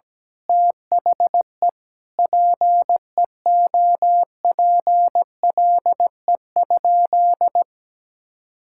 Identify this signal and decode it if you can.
Morse code, heard as a keyed tone, about 17 wpm, 700 Hz. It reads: THE PEOPLE?
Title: Sentences from Top 1000 Words - Review Part 5 - 17wpm